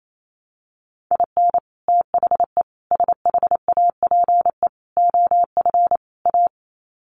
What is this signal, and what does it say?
Morse code, heard as a keyed tone, about 28 wpm, 700 Hz. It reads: IN THE SHAPE OF A
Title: Sets of 5 Words Encore 28wpm